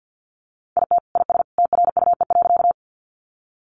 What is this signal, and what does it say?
Morse code, heard as a keyed tone, about 50 wpm, 700 Hz. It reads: IT IS TRUE.
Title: Sentences from Top 400 Words 50wpm